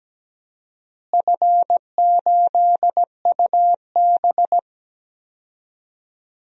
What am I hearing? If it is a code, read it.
Morse code, heard as a keyed tone, about 17 wpm, 700 Hz. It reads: F8UB
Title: DXCC-100 Call Signs - 4 Characters - Repeated 2x - 17wpm